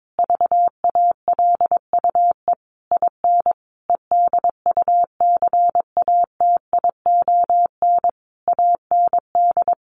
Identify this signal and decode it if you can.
Morse code, heard as a keyed tone, about 22 wpm, 700 Hz. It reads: VALUE IN EDUCATION AND
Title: Sentences from Top 1100 Words 22wpm